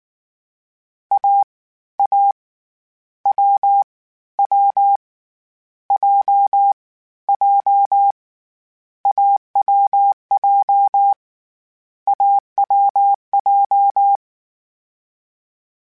Morse code, heard as a keyed tone, about 19 wpm, 800 Hz. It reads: A A W W J J AWJ AWJ
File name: Difficult Combos / Difficult Combos-mp3s / awj1 ndb6 gz7 @18